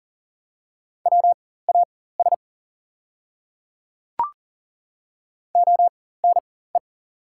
Morse code, heard as a keyed tone, about 40 wpm, spaced as for 19 wpm, 700 Hz. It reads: WAS ONE
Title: 4x Character Spacing - Top 100 Words - 40wpm